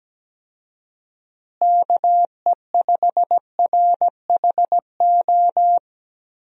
Morse code, heard as a keyed tone, about 17 wpm, 700 Hz. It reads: KE5RHO
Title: USA Call Signs - 6 Characters - Repeated 3x - 17wpm